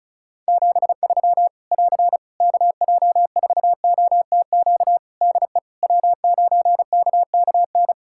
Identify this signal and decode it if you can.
Morse code, heard as a keyed tone, about 34 wpm, 700 Hz. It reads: 73 <AR> KJ4OTQ DE W9KKN
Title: Practice QSO - Exchange 4 - Ending - 35wpm